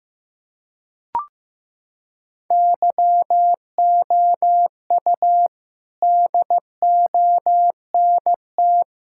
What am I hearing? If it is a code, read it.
Morse code, heard as a keyed tone, about 15 wpm, 700 Hz. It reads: YOU DONT
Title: Sentences from Top 1000 Words - Review Part 6 - 15wpm